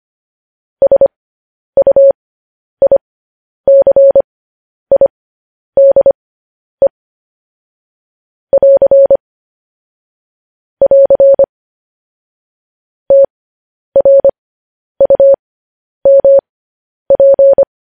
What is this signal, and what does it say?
Morse code, heard as a keyed tone, about 25 wpm, spaced as for 10 wpm, 550 Hz. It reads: SUICIDE <AR> <AR> TRUMP
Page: News Headlines in Morse Code at 10 WPM